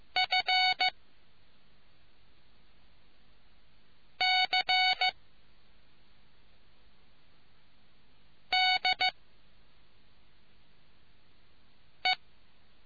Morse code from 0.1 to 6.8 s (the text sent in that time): F C